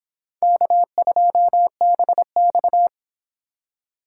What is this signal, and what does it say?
Morse code, heard as a keyed tone, about 29 wpm, 700 Hz. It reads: K2BX